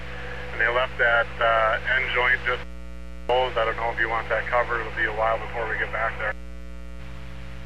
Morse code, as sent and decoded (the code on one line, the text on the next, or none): none